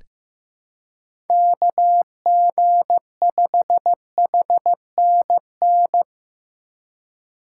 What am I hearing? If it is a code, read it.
Morse code, heard as a keyed tone, about 15 wpm, 700 Hz. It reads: KG5HNN
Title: USA Call Signs - 3x Character Spacing - 15wpm